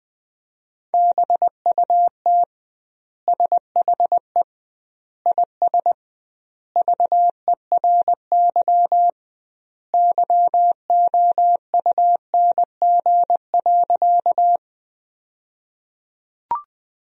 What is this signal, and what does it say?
Morse code, heard as a keyed tone, about 20 wpm, 700 Hz. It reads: BUT SHE IS VERY YOUNG.
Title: 2x Word-Spacing - Sentences from Top 400 Words - 20wpm